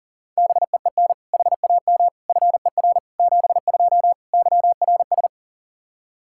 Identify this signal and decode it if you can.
Morse code, heard as a keyed tone, about 40 wpm, 700 Hz. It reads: BEEN HAM FER 72 YRS
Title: Practice QSO - Exchange 3 - More Info - 40wpm